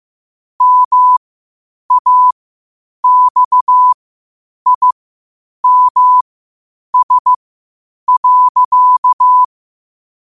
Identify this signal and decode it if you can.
Morse code, heard as a keyed tone, about 15 wpm, spaced as for 8 wpm, 1000 Hz. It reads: MAXIMS.
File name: qotdfiles / 8wpm